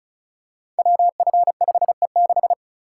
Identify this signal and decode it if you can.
Morse code, heard as a keyed tone, about 36 wpm, 690 Hz. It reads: WF5E6
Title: Code Groups - 5 Letter:Numbers - 35wpm